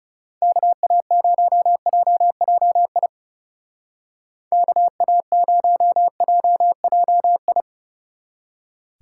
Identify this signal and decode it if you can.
Morse code, heard as a keyed tone, about 36 wpm, 700 Hz. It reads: KA0JJI KA0JJI
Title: Speed Racing - USA Call Signs 30wpm